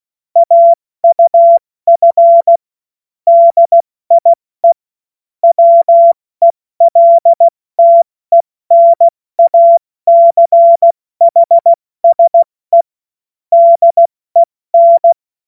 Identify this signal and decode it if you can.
Morse code, heard as a keyed tone, about 16 wpm, 680 Hz. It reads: AUF DIE WELTENACHSE DEN
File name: Morsekurs / Witze / Witz 16. Dezember 2025 70 F80